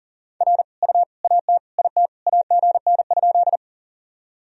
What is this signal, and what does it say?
Morse code, heard as a keyed tone, about 40 wpm, 700 Hz. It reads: R U AT IT AGN?
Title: Sentences from Top 200 Words and Abbreviations 40wpm